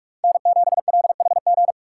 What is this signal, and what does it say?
Morse code, heard as a keyed tone, about 44 wpm, 700 Hz. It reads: N7LHG